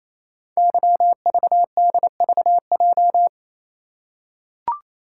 Morse code, heard as a keyed tone, about 28 wpm, 700 Hz. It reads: YVDVJ
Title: Code Groups - 5 Letters - 28wpm